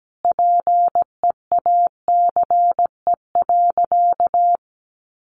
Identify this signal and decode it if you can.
Morse code, heard as a keyed tone, about 17 wpm, 700 Hz. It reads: PEACE.
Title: Sentences from Top 1500 Words 17wpm